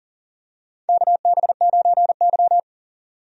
Morse code, heard as a keyed tone, about 40 wpm, 700 Hz. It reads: KB9Y
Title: USA Call Signs - 4 Characters - 40wpm